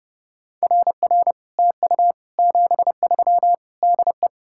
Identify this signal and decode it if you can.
Morse code, heard as a keyed tone, about 37 wpm, 700 Hz. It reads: RR TU 73 DE
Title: Speed-Racing Common QSO Phrases - 30wpm